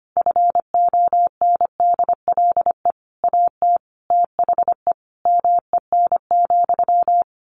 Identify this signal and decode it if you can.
Morse code, heard as a keyed tone, about 25 wpm, 700 Hz. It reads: FONDLE AT THE MEN,
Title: A Sound Of Thunder 25wpm